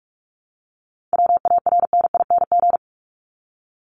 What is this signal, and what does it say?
Morse code, heard as a keyed tone, about 44 wpm, 700 Hz. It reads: WARNING
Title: Top 1751-2000 Words 45wpm